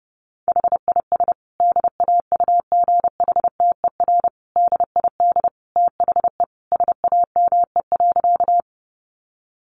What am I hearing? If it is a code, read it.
Morse code, heard as a keyed tone, about 30 wpm, 700 Hz. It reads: HIS DAUGHTER DID THE SAME.
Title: Sentences from Top 1500 Words 30wpm